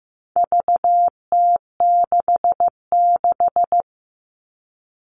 Morse code, heard as a keyed tone, about 15 wpm, 700 Hz. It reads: VT66
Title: All Cut Numbers - 4-Digit 15wpm